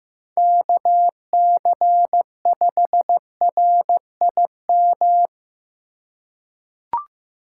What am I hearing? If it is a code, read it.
Morse code, heard as a keyed tone, about 15 wpm, 700 Hz. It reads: KC5RIM
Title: USA Call Signs - 6 Characters - 4x Character Spacing - 15wpm